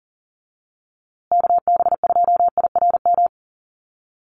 Morse code, heard as a keyed tone, about 40 wpm, 700 Hz. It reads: KB2IRM
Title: USA Call Signs Repeated 2x 40wpm